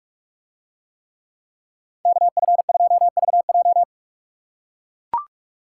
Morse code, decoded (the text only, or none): KF2VJ